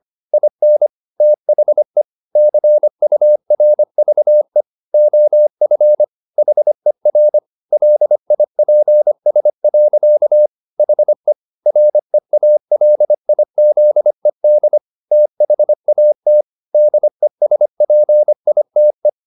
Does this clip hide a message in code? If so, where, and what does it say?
Yes: Morse code from 0.3 to 19.1 s, heard as a keyed tone, about 25 wpm, 600 Hz. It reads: IN THE CURVE OF HER LIPS. HE REALIZED THAT DESPITE